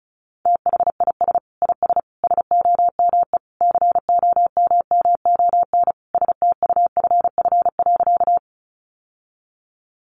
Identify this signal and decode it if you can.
Morse code, heard as a keyed tone, about 34 wpm, 700 Hz. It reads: THIS IS SOME COMMON STUFF.
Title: Sentences from Top 1400 Words 35wpm